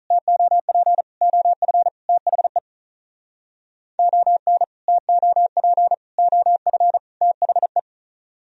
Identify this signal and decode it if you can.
Morse code, heard as a keyed tone, about 46 wpm, 700 Hz. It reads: TOP OF THE ON TOP OF THE